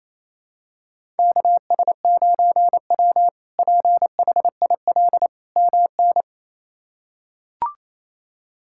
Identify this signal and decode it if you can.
Morse code, heard as a keyed tone, about 28 wpm, 700 Hz. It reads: KS9W PHIL MN
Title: K1USN SST QSO Practice - 28wpm